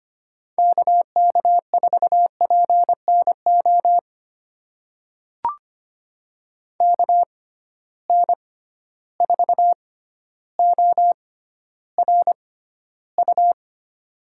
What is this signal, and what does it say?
Morse code, heard as a keyed tone, about 25 wpm, spaced as for 13 wpm, 700 Hz. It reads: KK4PNO KN4ORU
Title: USA Call Signs - 6 Characters - 6x Character Spacing - 25wpm